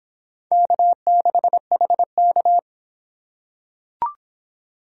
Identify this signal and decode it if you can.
Morse code, heard as a keyed tone, about 23 wpm, 700 Hz. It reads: K6HK